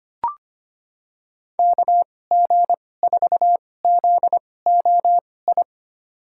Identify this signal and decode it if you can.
Morse code, heard as a keyed tone, about 25 wpm, spaced as for 18 wpm, 700 Hz. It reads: KG4ZOI
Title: USA Call Signs - 6 Characters - 2x Character Spacing - 25wpm